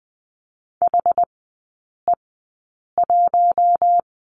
Morse code, heard as a keyed tone, about 20 wpm, spaced as for 8 wpm, 690 Hz. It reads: HE1
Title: 060 Lesson 11e - Letter h - Three Characters 20wpm